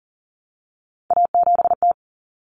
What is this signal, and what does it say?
Morse code, heard as a keyed tone, about 37 wpm, 700 Hz. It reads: A7T
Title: Common Cut Numbers - 3-Digit 40wpm